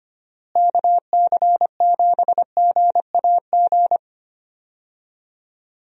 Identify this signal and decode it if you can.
Morse code, heard as a keyed tone, about 25 wpm, 700 Hz. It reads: KC7GAG
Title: USA Call Signs - 6 Characters - Repeated 2x - 25wpm